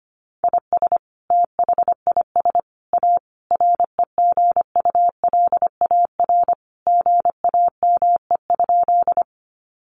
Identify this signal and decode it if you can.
Morse code, heard as a keyed tone, about 25 wpm, 700 Hz. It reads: IS THIS A REGULAR GAME?
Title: Sentences from Top 900 Words 25wpm